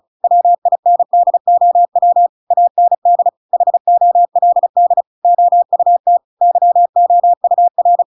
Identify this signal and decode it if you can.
Morse code, heard as a keyed tone, about 34 wpm, 700 Hz. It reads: WINDOW AND HOLD OUT YOUR